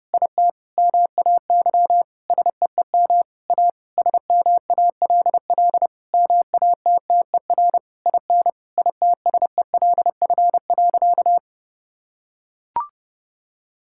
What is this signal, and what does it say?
Morse code, heard as a keyed tone, about 30 wpm, 700 Hz. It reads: IT MAY SEEM A SMALL MATTER IN ITSELF.
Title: Sentences from Top 1000 Words - Review Part 5 - 30wpm